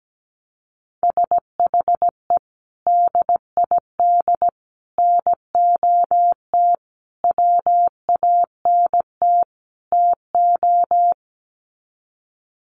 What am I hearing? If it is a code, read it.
Morse code, heard as a keyed tone, about 17 wpm, 700 Hz. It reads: SHE DID NOT WANT TO
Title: Sets of 5 Words 17wpm